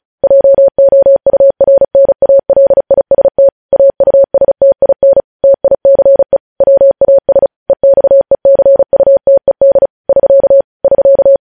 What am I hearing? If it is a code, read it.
Morse code, heard as a keyed tone, about 36 wpm, 550 Hz. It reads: JOURNALIST AUSTIN TICE WAS EXECUTED <SK> <SK>
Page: news headlines in morse code at 35 wpm